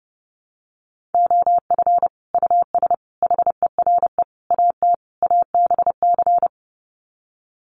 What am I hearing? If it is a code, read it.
Morse code, heard as a keyed tone, about 30 wpm, 700 Hz. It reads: OF US HERE AT ABC
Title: Sets of 5 Words 30wpm